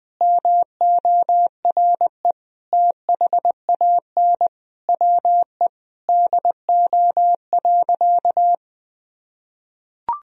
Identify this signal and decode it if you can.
Morse code, heard as a keyed tone, about 20 wpm, 700 Hz. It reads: MORE THAN WE DO.
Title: Sentences from Top 100 Words and Abbreviations 20wpm